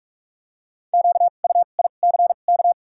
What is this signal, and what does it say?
Morse code, heard as a keyed tone, about 46 wpm, spaced as for 32 wpm, 700 Hz. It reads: QUICK
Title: 2x Character Spacing - Top 500 Words - 45wpm